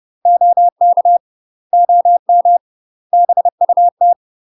Morse code, heard as a keyed tone, about 30 wpm, 700 Hz. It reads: OK OM BUT
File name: UK MASTER QSOs TRAINING / 30WPM / UK MASTER QSOs 0002